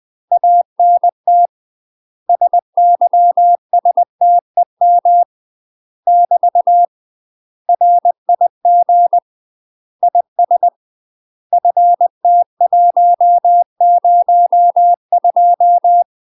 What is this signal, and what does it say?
Morse code, heard as a keyed tone, about 20 wpm, 700 Hz. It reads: ANT SYSTEM <BT> RIG IS FT102
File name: UK MASTER QSOs TRAINING / 20WPM / UK MASTER QSOs 0052